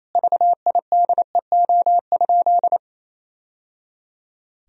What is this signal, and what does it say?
Morse code, heard as a keyed tone, about 28 wpm, 700 Hz. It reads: VIDEO?
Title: Sentences from Top 1000 Words - Review Part 4 - 28wpm